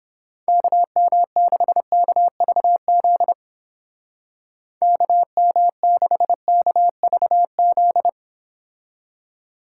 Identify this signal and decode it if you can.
Morse code, heard as a keyed tone, about 29 wpm, 700 Hz. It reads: KM6KVZ KM6KVZ